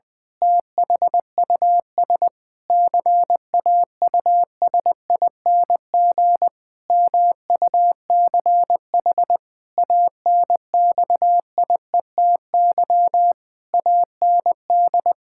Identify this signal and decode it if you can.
Morse code, heard as a keyed tone, about 20 wpm, 700 Hz. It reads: THUS CAUSING MUCH ANXIETY AND
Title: Pangram Challenge v2 - 20wpm